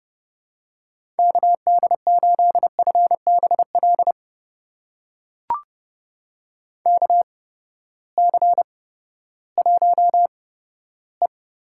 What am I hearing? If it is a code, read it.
Morse code, heard as a keyed tone, about 30 wpm, spaced as for 15 wpm, 700 Hz. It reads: KD8FBL KC1E
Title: USA Call Signs - 6 Characters - 8x Character Spacing - 30wpm